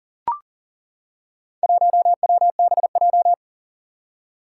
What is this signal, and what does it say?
Morse code, heard as a keyed tone, about 42 wpm, 700 Hz. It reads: 1WBJ
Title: Four Challenging Letter-Number Combinations 40wpm